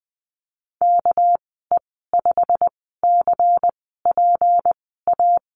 Morse code, heard as a keyed tone, about 20 wpm, spaced as for 15 wpm, 700 Hz. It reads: KE5CPA
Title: 200 Lesson 32g - Letter k - Call Signs 20wpm